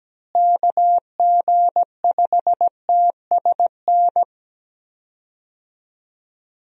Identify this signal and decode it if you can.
Morse code, heard as a keyed tone, about 17 wpm, 700 Hz. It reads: KG5TSN
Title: USA Call Signs - 6 Characters - Repeated 2x - 17wpm